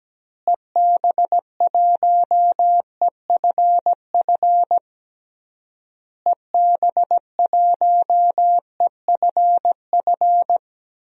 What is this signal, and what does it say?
Morse code, heard as a keyed tone, about 17 wpm, 700 Hz. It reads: EB1EFF EB1EFF
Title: DXCC-100 Call Signs - 6 Characters - Repeated 3x - 17wpm